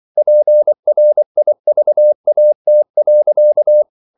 Morse code, heard as a keyed tone, about 24 wpm, 600 Hz. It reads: PRIVAT.
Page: Ferie 24 wpm